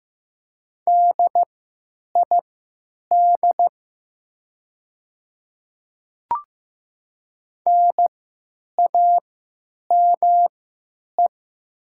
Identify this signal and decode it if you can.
Morse code, heard as a keyed tone, about 15 wpm, spaced as for 9 wpm, 700 Hz. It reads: DID NAME
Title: 3x Character Spacing - Top 300 Words - 15wpm